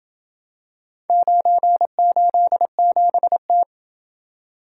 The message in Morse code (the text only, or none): 987T